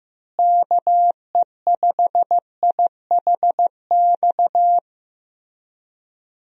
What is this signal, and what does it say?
Morse code, heard as a keyed tone, about 15 wpm, 700 Hz. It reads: KE5IHX
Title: USA Call Signs - 6 Characters - Repeated 3x - 15wpm